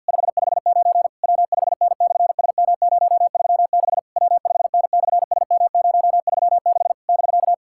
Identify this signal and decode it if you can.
Morse code, heard as a keyed tone, about 50 wpm, 700 Hz. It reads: 559 W5N/SM036 W5N/SM036 <BK>
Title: SOTA QSO Practice 50wpm